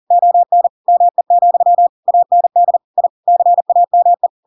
Morse code, heard as a keyed tone, about 40 wpm, 700 Hz. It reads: ON ME, AND I CAME
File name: Pinocchio / Pinocchio ch7 40 40 ik1hge